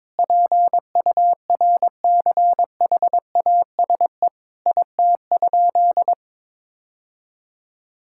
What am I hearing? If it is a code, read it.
Morse code, heard as a keyed tone, about 22 wpm, 700 Hz. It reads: PURCHASE IT?
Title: Sentences from Top 1000 Words - Review Part 5 - 22wpm